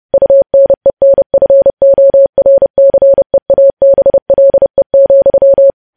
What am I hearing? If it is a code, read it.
Morse code, heard as a keyed tone, about 30 wpm, 550 Hz. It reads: UNENFORCEABLE,
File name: podcast / News headlines-1x-30x30